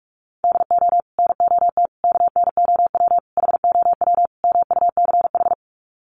Morse code, heard as a keyed tone, about 44 wpm, 700 Hz. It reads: DO NOT KNOW HOW MUCH
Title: Sets of 5 Words Encore 45wpm